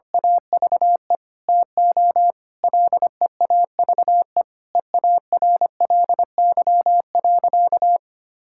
Morse code, heard as a keyed tone, about 25 wpm, 700 Hz. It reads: AVE TO LEAVE EARLY.
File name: practice / 25wpm / Sentences with 3-Word Phrases - 25wpm